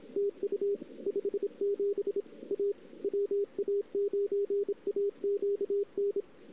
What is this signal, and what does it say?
Morse code, heard as a keyed tone, about 32 wpm, 400 Hz. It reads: TU 57 A WA9AQN
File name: CW / 2008-ss-cw-k5tr-op-n5rz -00:00